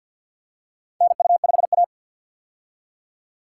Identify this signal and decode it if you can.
Morse code, heard as a keyed tone, about 53 wpm, 700 Hz. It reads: NU5A
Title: K1USN SST QSO Practice - 50wpm